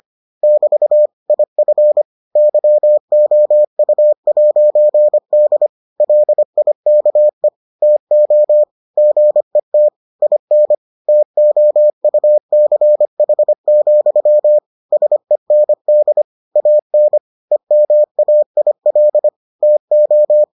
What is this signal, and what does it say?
Morse code, heard as a keyed tone, about 25 wpm, 600 Hz. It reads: <BT> IF YOU'D LIKE TO GET IN TOUCH, SEND AN EMAIL TO